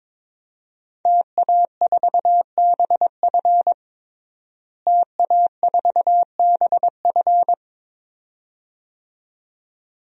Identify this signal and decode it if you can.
Morse code, heard as a keyed tone, about 21 wpm, 700 Hz. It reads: TA4BF TA4BF
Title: DXCC-100 Call Signs - 5 Characters - Repeated 2x - 22wpm